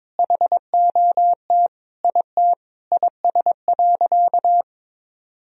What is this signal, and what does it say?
Morse code, heard as a keyed tone, about 21 wpm, 700 Hz. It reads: HOT IT IS.
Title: Sentences from Top 1000 Words - Review Part 3 - 22wpm